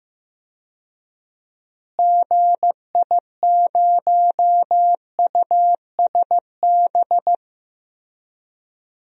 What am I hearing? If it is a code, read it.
Morse code, heard as a keyed tone, about 15 wpm, 700 Hz. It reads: GI0USB
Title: DXCC-100 Call Signs - 6 Characters - Repeated 3x - 15wpm